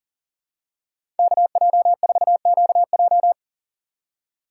Morse code, heard as a keyed tone, about 37 wpm, 700 Hz. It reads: KJ4QJ
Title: USA Call Signs - 5 Characters - 6x Character Spacing - 40wpm